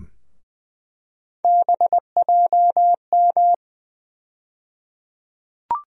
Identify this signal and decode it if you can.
Morse code, heard as a keyed tone, about 20 wpm, 700 Hz. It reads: BJM
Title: USA Call Sign Postfixes - 3 Letters 20wpm